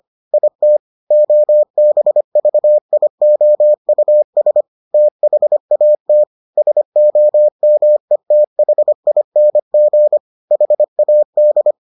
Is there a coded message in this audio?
Yes: Morse code from 0.3 to 11.7 s, heard as a keyed tone, about 25 wpm, 600 Hz. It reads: IT OBVIOUS THAT SOMETHING HAD